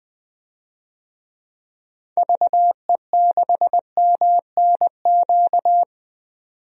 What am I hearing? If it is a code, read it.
Morse code, heard as a keyed tone, about 20 wpm, 700 Hz. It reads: VE6MNQ
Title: DXCC-100 Call Signs - 6 Characters - 20wpm